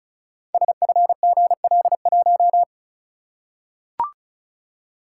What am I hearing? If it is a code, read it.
Morse code, heard as a keyed tone, about 35 wpm, 700 Hz. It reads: SFGL1
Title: Code Groups - 5 Letter:Numbers - 35wpm